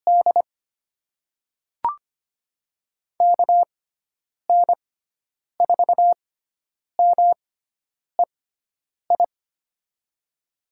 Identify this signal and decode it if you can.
Morse code, heard as a keyed tone, about 25 wpm, spaced as for 9 wpm, 700 Hz. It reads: D KN4MEI
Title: USA Call Signs - 6 Characters - 6x Character Spacing - 25wpm